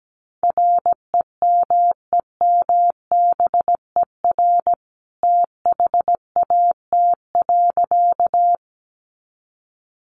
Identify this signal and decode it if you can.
Morse code, heard as a keyed tone, about 17 wpm, 700 Hz. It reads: REMEMBER THAT.
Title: A Sound Of Thunder 17wpm